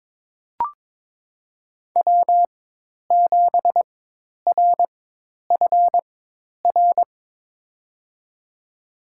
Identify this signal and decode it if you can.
Morse code, heard as a keyed tone, about 22 wpm, spaced as for 10 wpm, 700 Hz. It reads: W7RFR
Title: USA Call Signs - 5 Characters - 4x Character Spacing - 22wpm